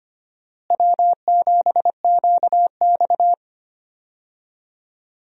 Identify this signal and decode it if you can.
Morse code, heard as a keyed tone, about 25 wpm, 700 Hz. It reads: W7QX
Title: USA Call Signs - 4 Characters - Repeated 2x - 25wpm